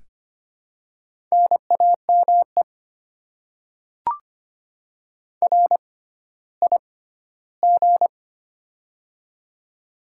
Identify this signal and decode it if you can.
Morse code, heard as a keyed tone, about 25 wpm, spaced as for 14 wpm, 700 Hz. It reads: NAME RIG
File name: Top 20 QSO Elements - 6x Character Spacing - 25wpm